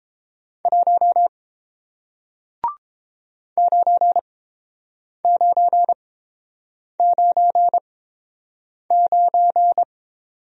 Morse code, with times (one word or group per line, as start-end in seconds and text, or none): 0.7-1.3 1
3.6-9.8 9999